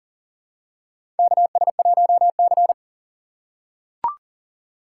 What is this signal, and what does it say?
Morse code, heard as a keyed tone, about 40 wpm, 700 Hz. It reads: KS1C
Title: USA Call Signs - 3x Character Spacing - 40wpm